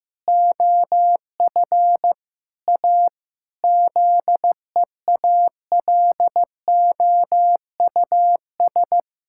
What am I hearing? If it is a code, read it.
Morse code, heard as a keyed tone, about 15 wpm, 700 Hz. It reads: OF A ZEALOUS
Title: Pangram Challenge v2 - 15wpm